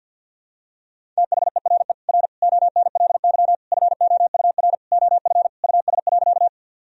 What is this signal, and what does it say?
Morse code, heard as a keyed tone, about 49 wpm, 700 Hz. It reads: THERE R ONLY FOUR OF US.
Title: Sentences from Top 300 Words and Abbreviations 50wpm